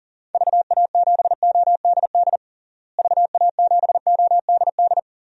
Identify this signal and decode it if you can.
Morse code, heard as a keyed tone, about 40 wpm, 700 Hz. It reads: VA7ODD VA7ODD
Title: DXCC-100 Call Signs - 6 Characters - Repeated 3x - 40wpm